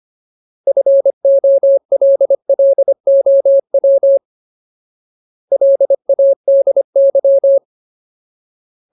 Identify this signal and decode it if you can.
Morse code, heard as a keyed tone, about 25 wpm, 550 Hz. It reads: FOLLOW LADY
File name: Ultimate Word List / 25WPM Ultimate